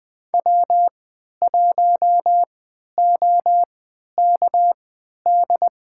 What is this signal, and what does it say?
Morse code, heard as a keyed tone, about 20 wpm, spaced as for 11 wpm, 700 Hz. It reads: W1OKD
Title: USA Call Signs - 5 Characters - 3x Character Spacing - 20wpm